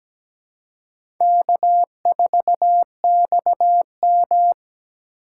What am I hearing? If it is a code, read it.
Morse code, heard as a keyed tone, about 17 wpm, 700 Hz. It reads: K4XM
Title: USA Call Signs - 4x Character Spacing - 17wpm